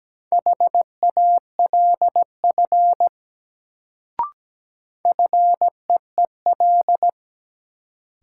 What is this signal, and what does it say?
Morse code, heard as a keyed tone, about 17 wpm, 700 Hz. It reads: HALF FEEL
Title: Sentences from Top 400 Words - New Word Drill - 17wpm